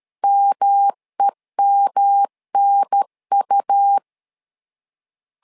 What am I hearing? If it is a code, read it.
Morse code, heard as a keyed tone, about 13 wpm, 800 Hz. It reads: MEMNU
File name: LEZIONI / 3 MWRO / 3 M 12wpm 1541